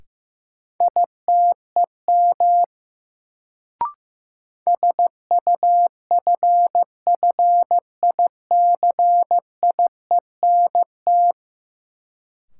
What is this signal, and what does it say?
Morse code, heard as a keyed tone, about 16 wpm, 700 Hz. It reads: ITEM SUFFICIENT
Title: Sentences from Top 1400 Words - New Word Drill Alt - 15wpm